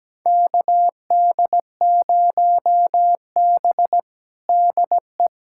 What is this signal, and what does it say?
Morse code, heard as a keyed tone, about 17 wpm, 700 Hz. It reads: KD0B DE
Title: USA Callsign DE Callsign - 4 Characters - 17wpm